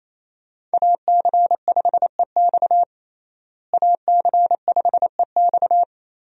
Morse code, heard as a keyed tone, about 28 wpm, 700 Hz. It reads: AC5EX AC5EX
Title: USA Call Signs - 5 Characters - Repeated 2x - 28wpm